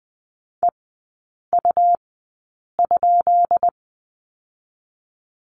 Morse code, heard as a keyed tone, about 20 wpm, 700 Hz. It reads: E U ?
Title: 198 Lesson 32e - Letter k - Three Characters 20wpm